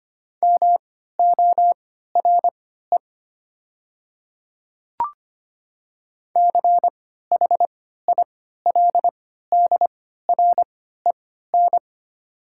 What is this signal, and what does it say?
Morse code, heard as a keyed tone, about 25 wpm, spaced as for 14 wpm, 700 Hz. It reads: MORE CHILDREN
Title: 3x Character Spacing - Top 300 Words - 25wpm